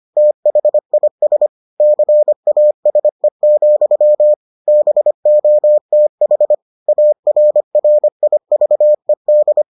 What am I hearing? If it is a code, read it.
Morse code, heard as a keyed tone, about 25 wpm, 600 Hz. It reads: THIS CASE, BOTH ARRIVED